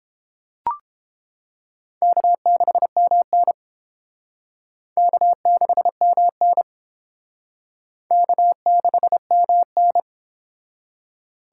Morse code, heard as a keyed tone, about 29 wpm, 700 Hz. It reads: K6MN K6MN K6MN